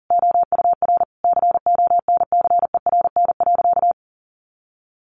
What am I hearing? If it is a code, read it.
Morse code, heard as a keyed tone, about 40 wpm, 700 Hz. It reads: OUR CONCERN.
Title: Sentences from Top 1400 Words 40wpm